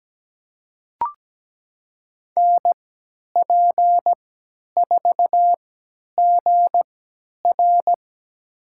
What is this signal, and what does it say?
Morse code, heard as a keyed tone, about 17 wpm, spaced as for 10 wpm, 700 Hz. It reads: NP4GR
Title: USA Call Signs - 3x Character Spacing - 17wpm